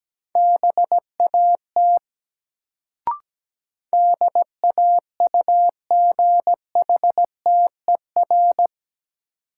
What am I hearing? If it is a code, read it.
Morse code, heard as a keyed tone, about 17 wpm, 700 Hz. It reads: BAT DAUGHTER
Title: Sentences from Top 1500 Words - New Word Drill Alt - 17wpm